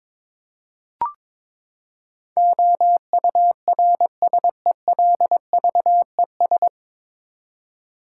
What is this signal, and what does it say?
Morse code, heard as a keyed tone, about 22 wpm, 700 Hz. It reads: OURSELVES
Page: Morse Code Ninja practice 22wpm